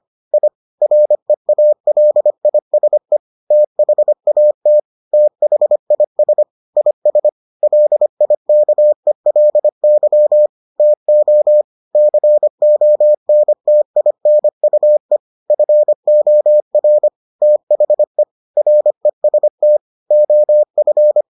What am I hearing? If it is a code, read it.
Morse code, heard as a keyed tone, about 25 wpm, 600 Hz. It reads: I REALISE THAT THIS IS LIKELY TO CONTINUE FOR THE REST OF